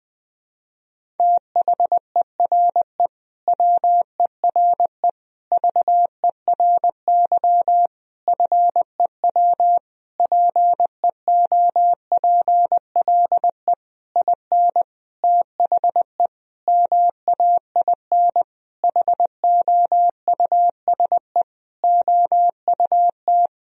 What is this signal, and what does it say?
Morse code, heard as a keyed tone, about 21 wpm, 700 Hz. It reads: THERE WERE VERY FEW PEOPLE IN THE MAIN HOUSE OUT
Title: Sentences from Top 1000 Words - Review Part 2 - 20wpm